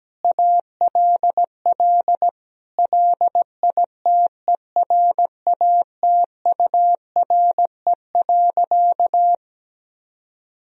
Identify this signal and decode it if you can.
Morse code, heard as a keyed tone, about 17 wpm, 700 Hz. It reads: ALL LITERATURE.
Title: Sentences from Top 1000 Words - Review Part 3 - 17wpm